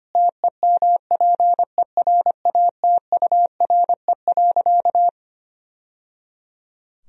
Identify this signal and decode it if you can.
Morse code, heard as a keyed tone, about 25 wpm, 700 Hz. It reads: TEMPERATURE.
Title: Sentences from Top 1000 Words - Review Part 4 - 25wpm